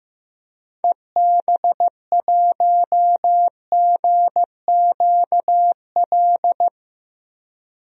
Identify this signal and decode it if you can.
Morse code, heard as a keyed tone, about 16 wpm, 700 Hz. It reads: EB1GQL
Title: DXCC-100 Call Signs - 6 Characters - 15wpm